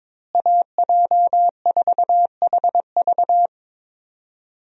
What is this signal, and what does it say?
Morse code, heard as a keyed tone, about 22 wpm, 700 Hz. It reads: AJ4HV
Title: USA Call Signs - 5 Characters - Repeated 3x - 22wpm